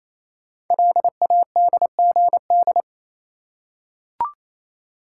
Morse code, heard as a keyed tone, about 27 wpm, 700 Hz. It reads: LADGD
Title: Code Groups - 5 Letters - 28wpm